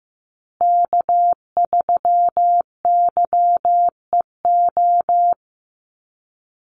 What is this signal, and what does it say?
Morse code, heard as a keyed tone, about 15 wpm, 700 Hz. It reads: K3YEO
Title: Call Signs Repeated 2x - 15wpm